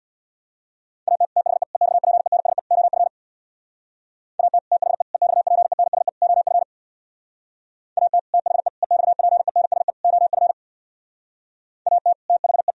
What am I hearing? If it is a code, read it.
Morse code, heard as a keyed tone, about 64 wpm, 700 Hz. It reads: AT THE EXPENSE OF AT THE EXPENSE OF AT THE EXPENSE OF AT THE